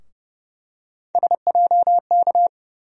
Morse code, heard as a keyed tone, about 31 wpm, 700 Hz. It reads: SJK
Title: USA Call Sign Postfixes - 3 Letters 30wpm